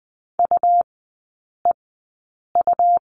Morse code, heard as a keyed tone, about 19 wpm, spaced as for 8 wpm, 700 Hz. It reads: UEU
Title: 100 Lesson 17e - Letter u - Three Characters 20wpm